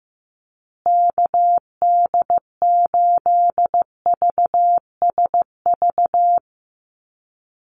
Morse code, heard as a keyed tone, about 15 wpm, 700 Hz. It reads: KD8VSV
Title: USA Call Signs Repeated 3x 15wpm